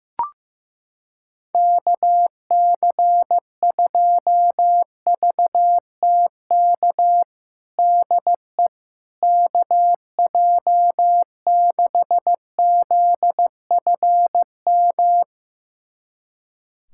Morse code, heard as a keyed tone, about 15 wpm, 700 Hz. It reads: KC2VTK DE KJ6ZFM
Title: USA Callsign DE Callsign 15wpm